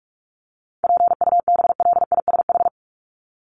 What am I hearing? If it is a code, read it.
Morse code, heard as a keyed tone, about 44 wpm, 700 Hz. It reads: PUBLISH
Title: ED Suffix - Top 100 Words with Base Words 45wpm